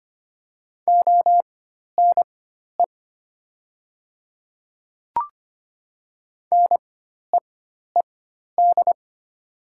Morse code, heard as a keyed tone, about 25 wpm, spaced as for 12 wpm, 700 Hz. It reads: ONE NEED
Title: 4x Character Spacing - Top 200 Words - 25wpm